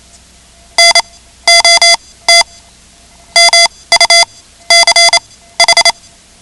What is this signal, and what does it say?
Morse code, heard as a keyed tone, about 28 wpm, spaced as for 16 wpm, 2050 Hz. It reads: NOT MUCH
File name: ham radio / CodeQuick / Code Quick 10